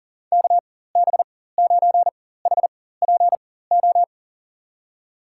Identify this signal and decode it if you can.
Morse code, heard as a keyed tone, about 40 wpm, spaced as for 19 wpm, 700 Hz. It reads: KB9HPO
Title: USA Call Signs - 6 Characters - 4x Character Spacing - 40wpm